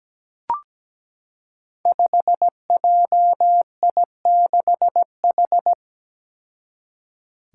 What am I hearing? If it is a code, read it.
Morse code, heard as a keyed tone, about 17 wpm, 700 Hz. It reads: 5JI6H
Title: Five Challenging Letter-Number Combinations 17wpm